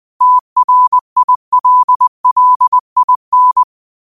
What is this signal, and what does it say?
Morse code, heard as a keyed tone, about 20 wpm, 1000 Hz. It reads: TRILLIN